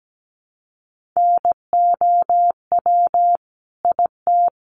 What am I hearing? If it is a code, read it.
Morse code, heard as a keyed tone, about 17 wpm, 700 Hz. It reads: NOW IT
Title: Sentences from Top 200 Words 17wpm